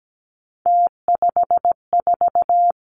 Morse code, heard as a keyed tone, about 17 wpm, 700 Hz. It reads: T54
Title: Three Challenging Letter-Number Combinations 17wpm